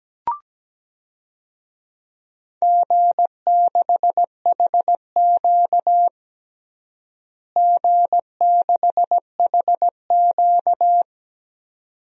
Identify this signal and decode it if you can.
Morse code, heard as a keyed tone, about 17 wpm, 700 Hz. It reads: G6HQ G6HQ
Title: DXCC-100 Call Signs - 4 Characters - Repeated 2x - 17wpm